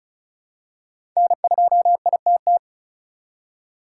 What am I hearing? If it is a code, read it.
Morse code, heard as a keyed tone, about 36 wpm, 700 Hz. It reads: N2ITT
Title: USA Call Signs - 5 Characters - 2x Character Spacing - 35wpm